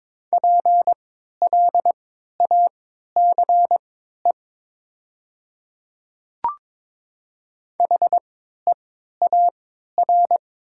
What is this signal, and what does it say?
Morse code, heard as a keyed tone, about 22 wpm, spaced as for 12 wpm, 700 Hz. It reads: PLACE HEAR
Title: 3x Character Spacing - Top 500 Words - 22wpm